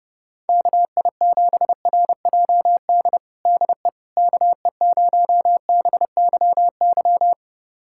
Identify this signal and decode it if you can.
Morse code, heard as a keyed tone, about 32 wpm, 700 Hz. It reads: KI7RJD DE KE0BYY
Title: USA Callsign DE Callsign - 6 Characters - 30wpm